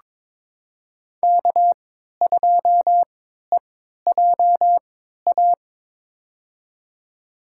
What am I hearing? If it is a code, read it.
Morse code, heard as a keyed tone, about 21 wpm, spaced as for 12 wpm, 700 Hz. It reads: K2EJA
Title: USA Call Signs - 5 Characters - 3x Character Spacing - 22wpm